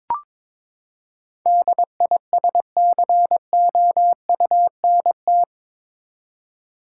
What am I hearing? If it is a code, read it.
Morse code, heard as a keyed tone, about 22 wpm, 700 Hz. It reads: DISCOUNT
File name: practice / 22wpm / Eight Letter Words - 22wpm